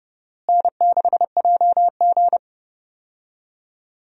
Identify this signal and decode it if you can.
Morse code, heard as a keyed tone, about 30 wpm, 700 Hz. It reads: N6JG
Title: USA Call Signs - 4 Characters - Repeated 2x - 30wpm